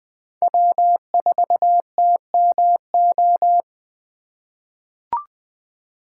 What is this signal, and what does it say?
Morse code, heard as a keyed tone, about 21 wpm, 700 Hz. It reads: W4TMO
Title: K1USN SST QSO Practice - 20wpm